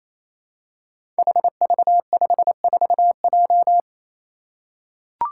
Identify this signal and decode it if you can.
Morse code, heard as a keyed tone, about 28 wpm, 700 Hz. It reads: HV54J
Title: Five Challenging Letter-Number Combinations 28wpm